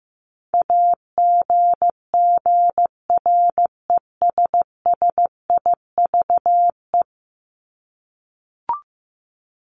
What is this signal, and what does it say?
Morse code, heard as a keyed tone, about 15 wpm, 700 Hz. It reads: AGGRESSIVE
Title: Top 1501-1750 Words 15wpm